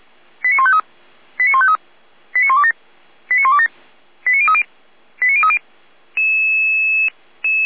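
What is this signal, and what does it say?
Five-tone selective call (ZVEI), heard as quick runs of five tones, tones of about 70 ms. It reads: 89153 89153 89127 89127 89030 89030
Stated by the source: ZVEI five-tone selective call received by SDR.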